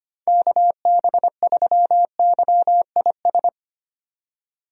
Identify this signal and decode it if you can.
Morse code, heard as a keyed tone, about 25 wpm, 700 Hz. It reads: KB3YIS
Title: USA Call Signs - 6 Characters - 25wpm